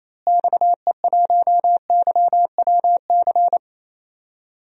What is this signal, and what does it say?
Morse code, heard as a keyed tone, about 28 wpm, 700 Hz. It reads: XE1YWC
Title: DXCC-100 Call Signs - 6 Characters - Repeated 3x - 28wpm